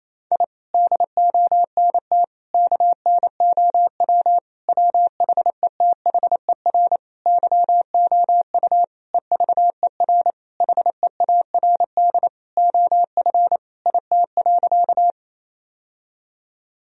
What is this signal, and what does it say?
Morse code, heard as a keyed tone, about 28 wpm, 700 Hz. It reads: I DONT KNOW WHETHER YOU EVER HEARD OF IT.
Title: Sentences from Top 1000 Words - Review Part 3 - 28wpm